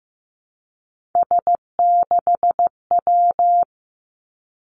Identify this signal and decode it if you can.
Morse code, heard as a keyed tone, about 15 wpm, 700 Hz. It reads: S6W
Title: Three Challenging Letter-Number Combinations 15wpm